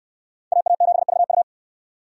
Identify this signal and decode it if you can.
Morse code, heard as a keyed tone, about 65 wpm, 700 Hz. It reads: AA7VU